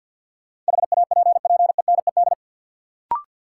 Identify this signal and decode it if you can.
Morse code, heard as a keyed tone, about 53 wpm, 700 Hz. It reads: HAPPENED
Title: Sentences from Top 1000 Words - Review Part 2 - New Word Drill - 50wpm